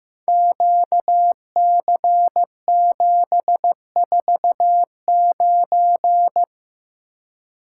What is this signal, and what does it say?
Morse code, heard as a keyed tone, about 15 wpm, 700 Hz. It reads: QC749
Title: Code Groups - 5 Letter:Numbers - 15wpm